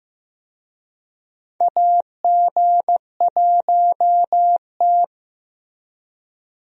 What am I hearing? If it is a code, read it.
Morse code, heard as a keyed tone, about 15 wpm, 700 Hz. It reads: AG1T
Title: USA Call Signs - 4 Characters - Repeated 3x - 15wpm